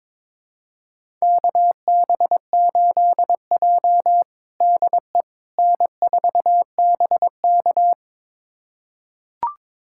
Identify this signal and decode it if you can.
Morse code, heard as a keyed tone, about 22 wpm, 700 Hz. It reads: KB8J DE N4BK
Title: USA Callsign DE Callsign - 4 Characters - 22wpm